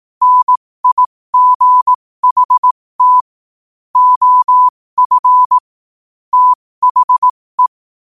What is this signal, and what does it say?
Morse code, heard as a keyed tone, about 18 wpm, spaced as for 15 wpm, 1000 Hz. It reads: NIGHT OF THE
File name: qotdfiles / 15wpm